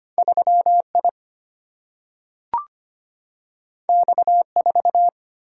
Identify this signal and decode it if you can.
Morse code, heard as a keyed tone, about 25 wpm, 700 Hz. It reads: 3I X4
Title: Single-Letter Single-Number 25wpm